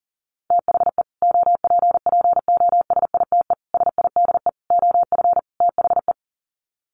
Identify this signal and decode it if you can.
Morse code, heard as a keyed tone, about 40 wpm, 700 Hz. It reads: THE OPPOSITE SIDE OF THE
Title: Sets of 5 Words Encore 40wpm